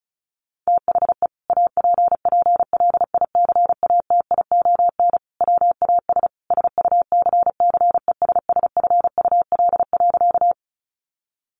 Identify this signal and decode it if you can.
Morse code, heard as a keyed tone, about 34 wpm, 700 Hz. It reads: THE APPLICATION WAS SUCCESSFUL.
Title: Sentences from Top 1200 Words 35wpm